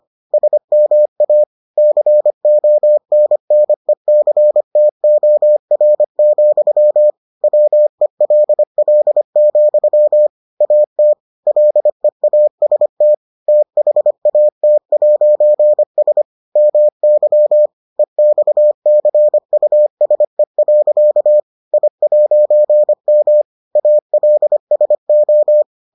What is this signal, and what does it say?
Morse code, heard as a keyed tone, about 25 wpm, 600 Hz. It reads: SMA CONNECTOR, WELL, AT LEAST THAT'S MY EXCUSE. I'M ALSO